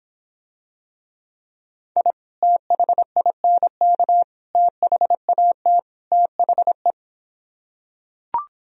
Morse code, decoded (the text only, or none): I THINK THAT THE